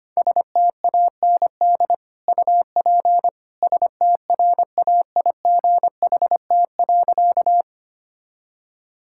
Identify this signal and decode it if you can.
Morse code, heard as a keyed tone, about 25 wpm, 700 Hz. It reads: STAND UP STRAIGHT.
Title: Sentences from Top 1000 Words - Review Part 6 - 25wpm